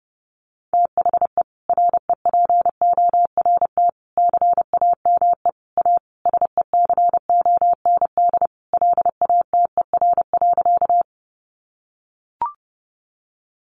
Morse code, heard as a keyed tone, about 28 wpm, 700 Hz. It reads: THE REPORT CAME A SECOND LATER.
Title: Sentences from Top 900 Words 30wpm